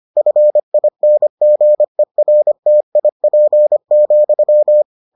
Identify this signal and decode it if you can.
Morse code, heard as a keyed tone, about 25 wpm, 600 Hz. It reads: FINGERTIP,